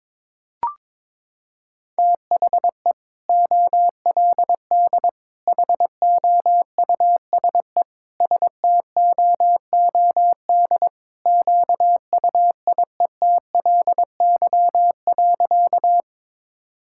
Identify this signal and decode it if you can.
Morse code, heard as a keyed tone, about 22 wpm, 700 Hz. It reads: THE OLD HOUSE STOOD QUIETLY.
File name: practice / 22wpm / Sentences with 2-Word Phrases - 22wpm